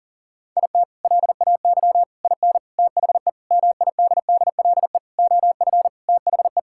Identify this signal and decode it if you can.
Morse code, heard as a keyed tone, about 40 wpm, 700 Hz. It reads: IT LAY IN THE MIDDLE OF THE
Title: Sentences from Top 1000 Words - Review Part 6 - 40wpm